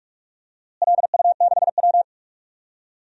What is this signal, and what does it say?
Morse code, heard as a keyed tone, about 42 wpm, 700 Hz. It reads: LU6W
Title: DXCC-100 Call Signs - 4 Characters - Repeated 3x - 45wpm